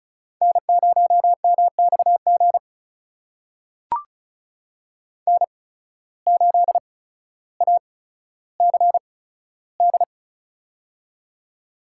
Morse code, heard as a keyed tone, about 36 wpm, spaced as for 15 wpm, 700 Hz. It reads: N0MXG N8ACD
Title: USA Call Signs - 5 Characters - 8x Character Spacing - 35wpm